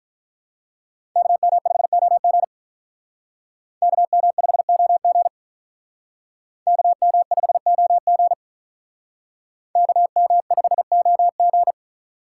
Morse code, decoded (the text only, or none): KM5OG KM5OG KM5OG KM5OG